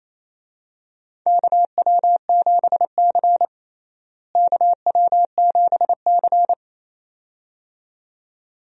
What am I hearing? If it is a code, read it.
Morse code, heard as a keyed tone, about 28 wpm, 700 Hz. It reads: KW7C KW7C
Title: USA Call Signs - 4 Characters - Repeated 2x - 28wpm